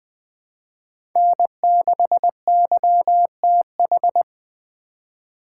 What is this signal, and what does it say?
Morse code, heard as a keyed tone, about 21 wpm, 700 Hz. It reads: N6YTH
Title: USA Call Signs - 5 Characters - 2x Character Spacing - 20wpm